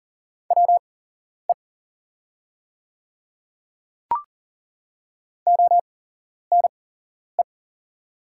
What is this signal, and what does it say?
Morse code, heard as a keyed tone, about 40 wpm, spaced as for 11 wpm, 700 Hz. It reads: WE ONE
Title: 8x Character Spacing - Top 100 Words - 40wpm